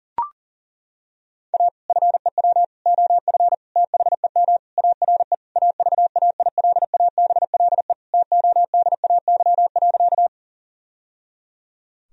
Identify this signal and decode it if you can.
Morse code, heard as a keyed tone, about 37 wpm, 700 Hz. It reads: A FEW OF THEM ARE AVAILABLE TODAY.
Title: Sentences from Top 1000 Words - Review Part 3 - 40wpm